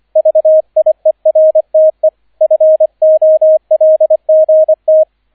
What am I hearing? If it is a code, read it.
Morse code, heard as a keyed tone, about 25 wpm, 630 Hz, with radio noise. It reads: VIERTE FOLGT
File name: 003 Dritter Streich 25wpm